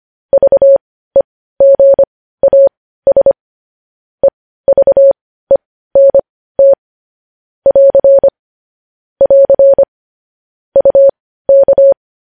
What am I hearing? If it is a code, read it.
Morse code, heard as a keyed tone, about 25 wpm, spaced as for 15 wpm, 550 Hz. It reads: VEGAS EVENT <AR> <AR> UK
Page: news headlines in morse code at 15 wpm